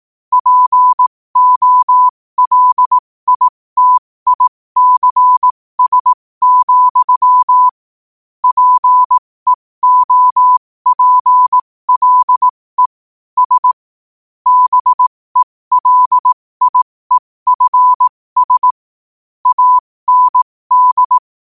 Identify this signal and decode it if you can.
Morse code, heard as a keyed tone, about 18 wpm, spaced as for 15 wpm, 1000 Hz. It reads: POLITICS, PEOPLE S BELIEFS AND
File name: qotdfiles / 15wpm